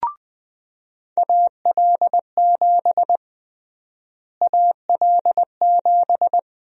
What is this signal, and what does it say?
Morse code, heard as a keyed tone, about 20 wpm, 700 Hz. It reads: AL7 AL7
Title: USA Call Sign Prefixes - Repeated 2x - 20wpm